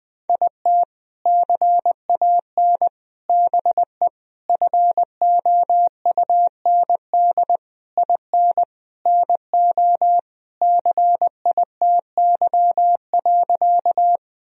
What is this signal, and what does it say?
Morse code, heard as a keyed tone, about 20 wpm, 700 Hz. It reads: IT CAN BE FOUND IN NO CITY.
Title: Sentences from Top 1000 Words - Review Part 1 - 20wpm